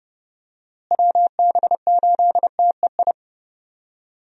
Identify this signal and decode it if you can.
Morse code, heard as a keyed tone, about 30 wpm, 700 Hz. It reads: WB8TEI
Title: USA Call Signs - 6 Characters - Repeated 2x - 30wpm